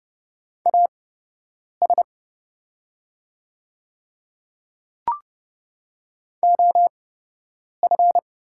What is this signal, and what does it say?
Morse code, heard as a keyed tone, about 30 wpm, spaced as for 8 wpm, 700 Hz. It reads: AS OF
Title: 8x Character Spacing - Top 100 Words - 30wpm